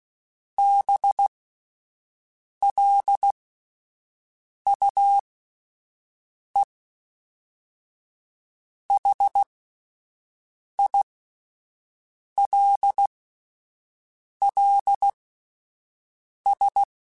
Morse code, heard as a keyed tone, about 16 wpm, spaced as for 5 wpm, 800 Hz. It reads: BLUEHILLS
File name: fromSMs / FromSMs mp3s / Cities@5